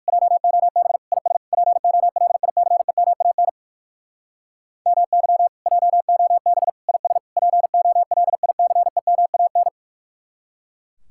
Morse code, heard as a keyed tone, about 59 wpm, 700 Hz. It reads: JOB IS POLICEMAN MY JOB IS POLICEMAN